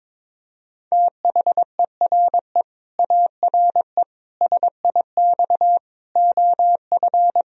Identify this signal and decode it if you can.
Morse code, heard as a keyed tone, about 22 wpm, 700 Hz. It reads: THERE ARE SIX OF
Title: Sentences from Top 1000 Words - Review Part 2 - 22wpm